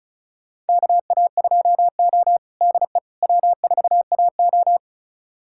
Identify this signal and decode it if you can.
Morse code, heard as a keyed tone, about 34 wpm, 700 Hz. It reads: KA2O DE W4AO
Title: USA Callsign DE Callsign - 4 Characters - 35wpm